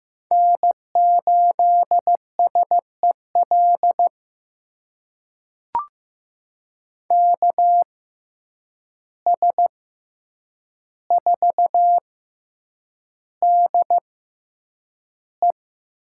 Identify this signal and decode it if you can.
Morse code, heard as a keyed tone, about 15 wpm, spaced as for 8 wpm, 700 Hz. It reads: N8SEL KS4DE
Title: USA Call Signs - 5 Characters - 6x Character Spacing - 15wpm